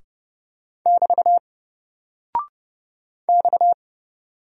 Morse code, heard as a keyed tone, about 29 wpm, 700 Hz. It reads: <BT> X
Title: Single Character (Letter:Number:Punctuation:Prosign) 30wpm